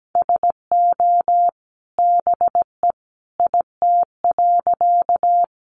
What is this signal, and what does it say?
Morse code, heard as a keyed tone, about 17 wpm, 700 Hz. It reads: SO BE IT.
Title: Sentences from Top 100 Words 17wpm